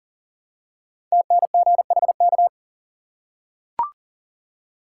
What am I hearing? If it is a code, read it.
Morse code, heard as a keyed tone, about 40 wpm, 700 Hz. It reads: TNGHK
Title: Code Groups - 5 Letter:Numbers - 40wpm